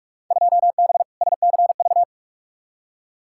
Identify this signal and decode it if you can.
Morse code, heard as a keyed tone, about 47 wpm, 700 Hz. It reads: 2B SCV
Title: Field Day Exchanges Encore - 45wpm